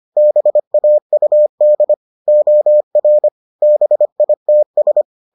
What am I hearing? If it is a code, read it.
Morse code, heard as a keyed tone, about 25 wpm, 600 Hz. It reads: BAUD OR BITS